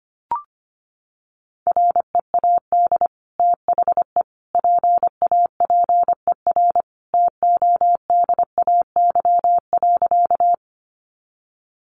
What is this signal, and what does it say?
Morse code, heard as a keyed tone, about 25 wpm, 700 Hz. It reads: READ THE PAPER TODAY.
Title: Sentences from Top 600 Words 25wpm